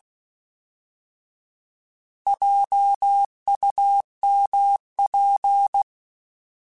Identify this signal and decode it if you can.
Morse code, heard as a keyed tone, about 16 wpm, 800 Hz. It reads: JUMP
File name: Alphabetical / Alphabetical mp3s / Js @15